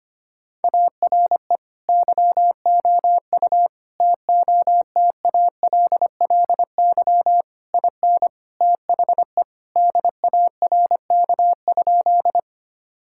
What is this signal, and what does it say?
Morse code, heard as a keyed tone, about 25 wpm, 700 Hz. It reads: ARE YOU TOTALLY IN THE DARK?
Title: Sentences from Top 1600 Words 25wpm